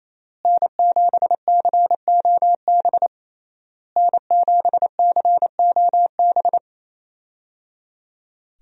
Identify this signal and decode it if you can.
Morse code, heard as a keyed tone, about 28 wpm, 700 Hz. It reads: N7COB N7COB
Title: USA Call Signs - 5 Characters - Repeated 3x - 28wpm